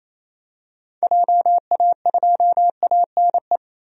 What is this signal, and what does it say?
Morse code, heard as a keyed tone, about 28 wpm, 700 Hz. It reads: JA2ANE
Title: DXCC-100 Call Signs - 6 Characters - Repeated 3x - 28wpm